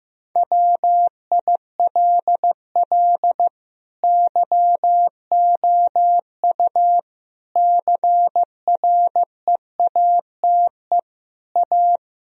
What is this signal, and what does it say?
Morse code, heard as a keyed tone, about 15 wpm, 700 Hz. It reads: WILL YOU CREATE A
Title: Sentences from Top 1000 Words - Review Part 3 - 15wpm